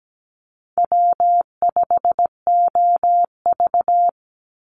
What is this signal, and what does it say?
Morse code, heard as a keyed tone, about 17 wpm, 700 Hz. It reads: W5OV
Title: Easy Call Signs - 17wpm